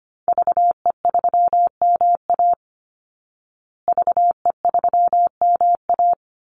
Morse code, heard as a keyed tone, about 25 wpm, 700 Hz. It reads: VE3MA VE3MA
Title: Call Signs Repeated 3x - 25wpm